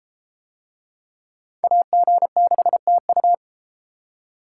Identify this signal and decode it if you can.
Morse code, heard as a keyed tone, about 33 wpm, 700 Hz. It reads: AG6TU